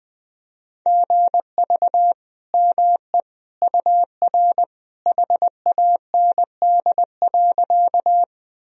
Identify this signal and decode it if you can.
Morse code, heard as a keyed tone, about 20 wpm, 700 Hz. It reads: GV ME UR HAND.
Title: Sentences from Top 200 Words and Abbreviations 20wpm